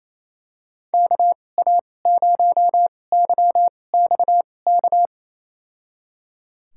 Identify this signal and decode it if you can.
Morse code, heard as a keyed tone, about 27 wpm, spaced as for 20 wpm, 700 Hz. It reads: KA0YXK
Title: USA Call Signs - 6 Characters - 2x Character Spacing - 28wpm